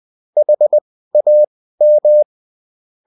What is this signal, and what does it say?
Morse code, heard as a keyed tone, about 20 wpm, spaced as for 14 wpm, 600 Hz. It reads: HAM